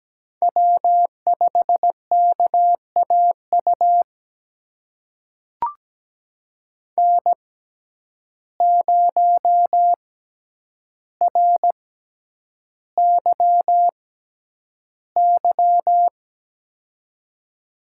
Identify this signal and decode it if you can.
Morse code, heard as a keyed tone, about 17 wpm, spaced as for 9 wpm, 700 Hz. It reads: W5KAU N0RYY
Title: USA Call Signs - 5 Characters - 6x Character Spacing - 17wpm